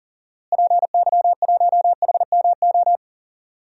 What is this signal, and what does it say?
Morse code, heard as a keyed tone, about 40 wpm, 700 Hz. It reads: PY1HMO
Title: DXCC-100 Call Signs - 6 Characters - Repeated 2x - 40wpm